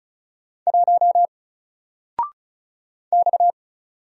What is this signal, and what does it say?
Morse code, heard as a keyed tone, about 36 wpm, 700 Hz. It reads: 1 X
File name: Single-Letter-Number-35wpm